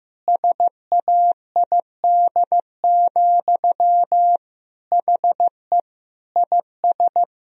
Morse code, heard as a keyed tone, about 16 wpm, 700 Hz. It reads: SAID, HE IS
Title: Sentences from Top 1000 Words - Review Part 3 - 15wpm